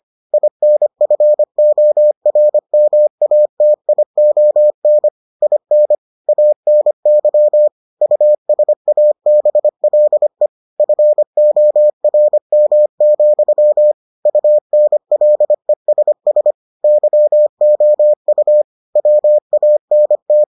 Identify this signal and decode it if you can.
Morse code, heard as a keyed tone, about 25 wpm, 600 Hz. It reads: INFORMATION IN ANY USABLE FORM, UNLESS YOU WANT